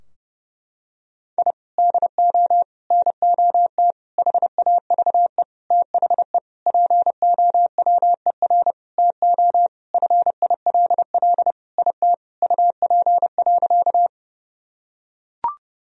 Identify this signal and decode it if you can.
Morse code, heard as a keyed tone, about 30 wpm, 700 Hz. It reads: I DO NOT HAVE THE POWER TO FILL IT UP.
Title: Sentences from Top 1000 Words - Review Part 3 - 30wpm